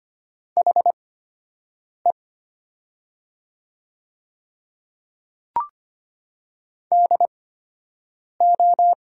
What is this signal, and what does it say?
Morse code, heard as a keyed tone, about 25 wpm, spaced as for 7 wpm, 700 Hz. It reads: HE DO
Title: 8x Character Spacing - Top 100 Words - 25wpm